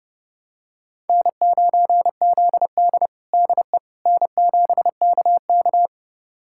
Morse code, heard as a keyed tone, about 31 wpm, 700 Hz. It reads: N9ZD DE N7KK
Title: USA Callsign DE Callsign - 4 Characters - 30wpm